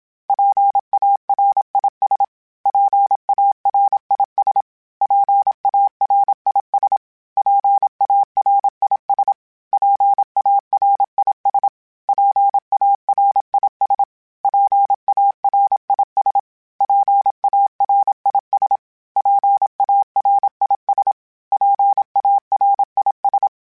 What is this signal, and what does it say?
Morse code, heard as a keyed tone, about 27 wpm, 800 Hz. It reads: PARIS PARIS PARIS PARIS PARIS PARIS PARIS PARIS PARIS PARIS
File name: PARIS Calibration / PARIS mp3s / PARIS@30